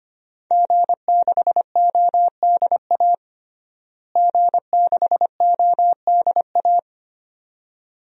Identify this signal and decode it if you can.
Morse code, heard as a keyed tone, about 25 wpm, 700 Hz. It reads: G6ODA G6ODA
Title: DXCC-100 Call Signs - 5 Characters - Repeated 2x - 25wpm